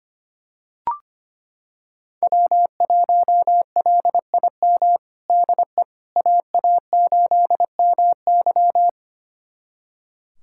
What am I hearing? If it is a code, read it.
Morse code, heard as a keyed tone, about 25 wpm, 700 Hz. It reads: W1LIM DE AA8MY
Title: USA Callsign DE Callsign - 5 Characters - 25wpm